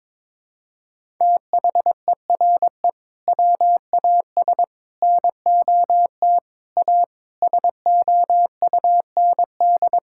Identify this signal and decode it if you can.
Morse code, heard as a keyed tone, about 22 wpm, 700 Hz. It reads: THERE WAS NOT A SOUND
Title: Sentences from Top 1000 Words - Review Part 6 - 22wpm